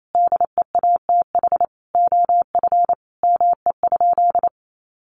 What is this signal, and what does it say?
Morse code, heard as a keyed tone, about 28 wpm, 700 Hz. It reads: DEATH OF ME?
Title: Sentences from Top 1100 Words 28wpm